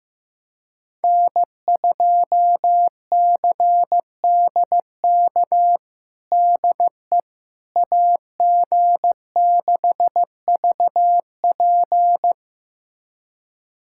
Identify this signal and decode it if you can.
Morse code, heard as a keyed tone, about 15 wpm, 700 Hz. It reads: N2CDK DE AG6VP
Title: USA Callsign DE Callsign - 5 Characters - 15wpm